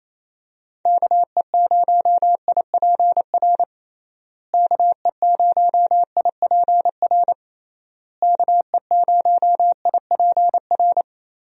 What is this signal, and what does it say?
Morse code, heard as a keyed tone, about 27 wpm, 700 Hz. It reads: KE0IPR KE0IPR KE0IPR
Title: USA Call Signs - 6 Characters - Repeated 3x - 28wpm